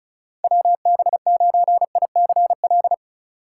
Morse code, heard as a keyed tone, about 36 wpm, 700 Hz. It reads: WB9ICL
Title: USA Call Signs - 6 Characters - 3x Character Spacing - 35wpm